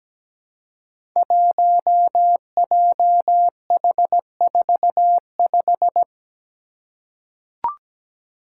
Morse code, heard as a keyed tone, about 17 wpm, 700 Hz. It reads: 1JH45
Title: Five Challenging Letter-Number Combinations 17wpm